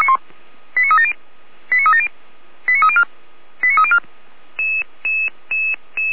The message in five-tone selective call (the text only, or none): 89370 89370 89304 89304